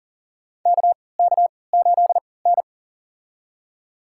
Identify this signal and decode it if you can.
Morse code, heard as a keyed tone, about 40 wpm, spaced as for 23 wpm, 700 Hz. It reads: KK8N
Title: USA Call Signs - 4 Characters - 3x Character Spacing - 40wpm